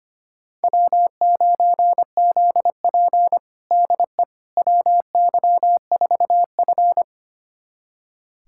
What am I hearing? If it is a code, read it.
Morse code, heard as a keyed tone, about 25 wpm, 700 Hz. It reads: W9ZP DE WY4F
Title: USA Callsign DE Callsign - 4 Characters - 25wpm